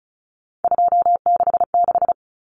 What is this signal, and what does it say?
Morse code, heard as a keyed tone, about 34 wpm, 700 Hz. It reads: 266
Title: Common Cut Numbers - 3-Digit 35wpm